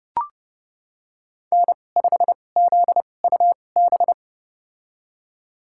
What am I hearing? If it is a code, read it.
Morse code, heard as a keyed tone, about 30 wpm, spaced as for 22 wpm, 700 Hz. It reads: N5ZUB
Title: USA Call Signs - 5 Characters - 2x Character Spacing - 30wpm